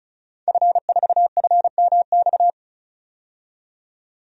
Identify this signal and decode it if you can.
Morse code, heard as a keyed tone, about 34 wpm, 700 Hz. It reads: F4FMX
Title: DXCC-100 Call Signs - 5 Characters - 35wpm